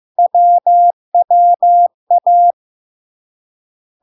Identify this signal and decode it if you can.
Morse code, heard as a keyed tone, about 15 wpm, 700 Hz. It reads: WWA
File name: LARK-FD-2022-Log-in-CW-15wpm-700-Hz